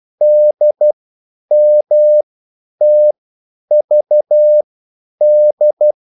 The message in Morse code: DMTVD

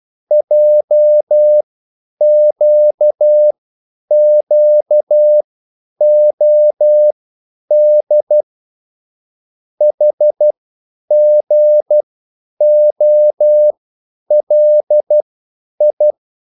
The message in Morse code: JQQOD HGOLI

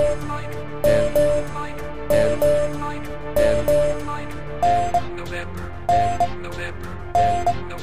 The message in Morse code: none

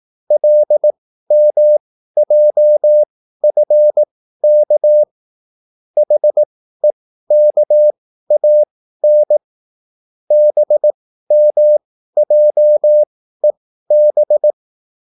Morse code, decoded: LMJFK HEKAN BMJEB

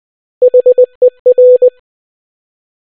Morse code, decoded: HER

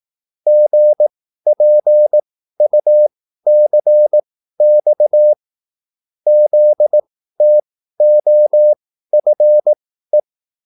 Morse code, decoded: GPUCX ZTOFE

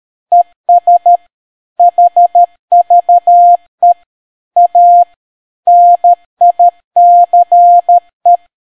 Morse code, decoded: ES HVE A NICE

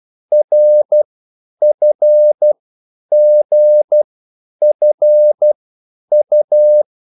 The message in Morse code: RFGFU

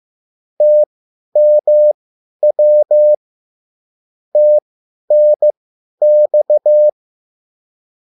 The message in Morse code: TMW TNX

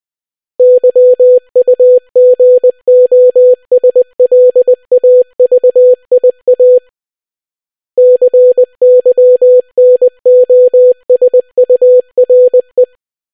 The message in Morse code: YUGOSLAVIA CYNOSURE

